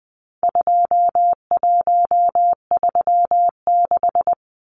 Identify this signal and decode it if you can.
Morse code, heard as a keyed tone, about 20 wpm, 700 Hz. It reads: 2136
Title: Numbers - 4-Digit 20wpm